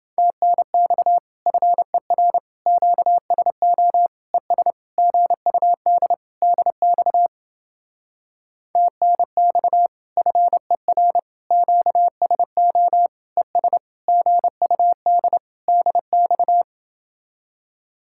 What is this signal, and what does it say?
Morse code, heard as a keyed tone, about 29 wpm, 700 Hz. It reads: TNX FER QSO ES GUD DX TNX FER QSO ES GUD DX